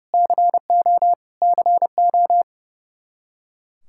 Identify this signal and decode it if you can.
Morse code, heard as a keyed tone, about 30 wpm, 700 Hz. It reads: CO CO
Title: K1USN SST QSO Practice 30wpm